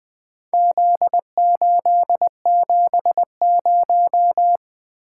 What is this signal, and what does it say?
Morse code, heard as a keyed tone, about 20 wpm, 700 Hz. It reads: Z870
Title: Four Letter-Number Combinations - 20wpm